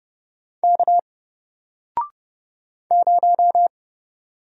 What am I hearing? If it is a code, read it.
Morse code, heard as a keyed tone, about 30 wpm, 700 Hz. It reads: K 0
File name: Single-Letter-Number-30wpm